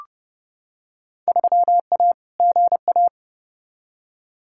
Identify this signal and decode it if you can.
Morse code, heard as a keyed tone, about 30 wpm, 700 Hz. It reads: 3A GA
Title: Field Day Exchanges Encore - 30wpm